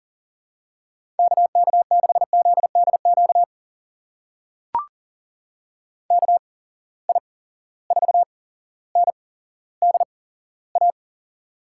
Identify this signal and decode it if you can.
Morse code, heard as a keyed tone, about 40 wpm, spaced as for 17 wpm, 700 Hz. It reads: KK6ZDQ KI4NDA
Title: USA Call Signs - 6 Characters - 8x Character Spacing - 40wpm